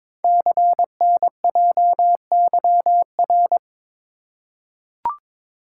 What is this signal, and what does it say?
Morse code, heard as a keyed tone, about 22 wpm, 700 Hz. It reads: CNJYR
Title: Code Groups - 5 Letters - 22wpm